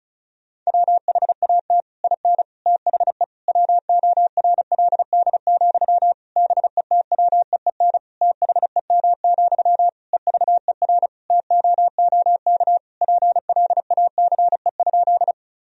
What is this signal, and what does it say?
Morse code, heard as a keyed tone, about 36 wpm, 700 Hz. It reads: WHAT IN THE WORLD, BETWEEN THEM, EVER TOOK PLACE?
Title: Sentences from Top 1000 Words - Review Part 1 - 35wpm